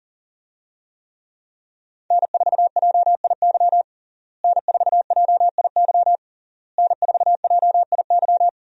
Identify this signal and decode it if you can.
Morse code, heard as a keyed tone, about 37 wpm, 700 Hz. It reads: N4JIY N4JIY N4JIY
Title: USA Call Signs - 5 Characters - Repeated 3x - 40wpm